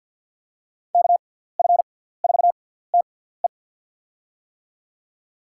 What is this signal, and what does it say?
Morse code, heard as a keyed tone, about 50 wpm, spaced as for 17 wpm, 700 Hz. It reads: KF4TE
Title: USA Call Signs - 5 Characters - 6x Character Spacing - 50wpm